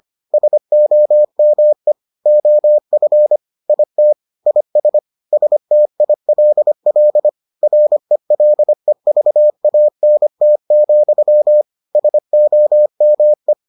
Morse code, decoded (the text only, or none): SOME OF IT IS STILL RELEVANT, SOME